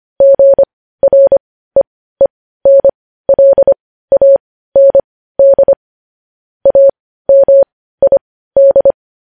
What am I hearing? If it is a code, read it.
Morse code, heard as a keyed tone, about 25 wpm, spaced as for 15 wpm, 550 Hz. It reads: GREENLAND AMID
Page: news-headlines-in-morse-code-at-15-wpm